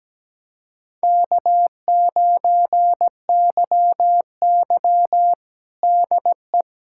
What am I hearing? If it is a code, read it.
Morse code, heard as a keyed tone, about 17 wpm, 700 Hz. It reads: K9YY DE
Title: USA Callsign DE Callsign - 4 Characters - 17wpm